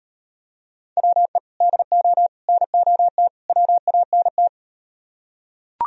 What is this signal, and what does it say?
Morse code, heard as a keyed tone, about 38 wpm, 700 Hz. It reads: WE DO NOT WANT